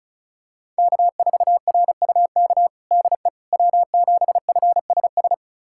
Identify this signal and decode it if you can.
Morse code, heard as a keyed tone, about 34 wpm, 700 Hz. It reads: K4RUK DE W7FSS
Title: USA Callsign DE Callsign - 5 Characters - 35wpm